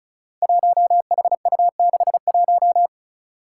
Morse code, heard as a keyed tone, about 36 wpm, 700 Hz. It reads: 1HU61
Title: Five Challenging Letter-Number Combinations 35wpm